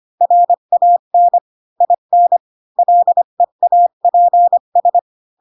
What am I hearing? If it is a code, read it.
Morse code, heard as a keyed tone, about 25 wpm, 700 Hz. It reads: RAN IN LEAPS
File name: Pinocchio / Pinocchio ch3 20 25 ik1hge